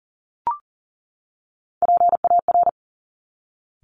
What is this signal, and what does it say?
Morse code, heard as a keyed tone, about 42 wpm, 700 Hz. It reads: PAR
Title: Top Three Letter Combinations in Words 40wpm